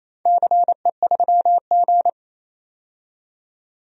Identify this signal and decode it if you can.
Morse code, heard as a keyed tone, about 28 wpm, 700 Hz. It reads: CE3G
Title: DXCC-100 Call Signs - 4 Characters - 28wpm